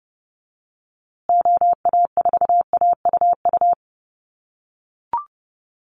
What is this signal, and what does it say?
Morse code, heard as a keyed tone, about 30 wpm, 700 Hz. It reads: OA4AUU
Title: USA Call Signs Repeated 2x 30wpm